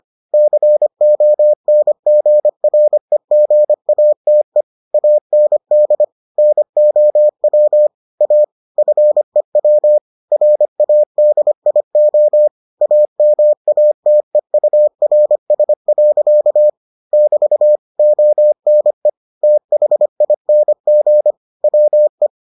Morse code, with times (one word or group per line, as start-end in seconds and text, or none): 0.3-4.6 CONGREGATE
4.9-6.0 AND
6.4-7.9 NOW
8.2-8.4 A
8.8-10.0 FEW
10.3-12.5 RADIO
12.8-16.7 AMATEURS.
17.1-17.8 <BT>
18.0-19.1 ONE
19.4-21.3 THING
21.6-22.3 WE